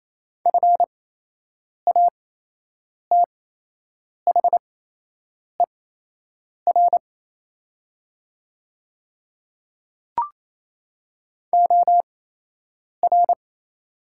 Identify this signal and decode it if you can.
Morse code, heard as a keyed tone, about 27 wpm, spaced as for 8 wpm, 700 Hz. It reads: FATHER OR
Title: 8x Character Spacing - Top 200 Words - 28wpm